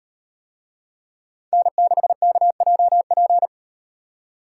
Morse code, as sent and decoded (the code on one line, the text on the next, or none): -. -.... -.- .--- .--.
N6KJP